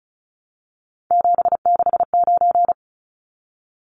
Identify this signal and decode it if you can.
Morse code, heard as a keyed tone, about 34 wpm, 700 Hz. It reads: 769
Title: Numbers - 3-Digit 35wpm